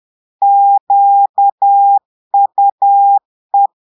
Morse code, heard as a keyed tone, about 10 wpm, 800 Hz. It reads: QUE
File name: Portugais - texte en clair 1 - vitesse - 10 mots minute